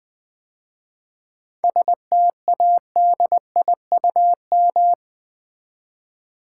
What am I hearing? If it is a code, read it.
Morse code, heard as a keyed tone, about 20 wpm, 700 Hz. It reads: STADIUM
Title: 7 Letter Words - First 3 Letters Given - 20wpm